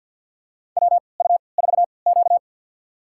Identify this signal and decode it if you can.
Morse code, heard as a keyed tone, about 47 wpm, spaced as for 28 wpm, 700 Hz. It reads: WU4Q
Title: USA Call Signs - 4 Characters - 3x Character Spacing - 50wpm